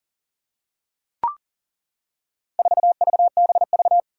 Morse code, heard as a keyed tone, about 40 wpm, 700 Hz. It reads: 4VBV
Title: Four Challenging Letter-Number Combinations 40wpm